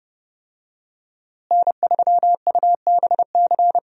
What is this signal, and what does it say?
Morse code, heard as a keyed tone, about 30 wpm, 700 Hz. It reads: N3UBC
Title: USA Call Signs - 5 Characters - Repeated 3x - 30wpm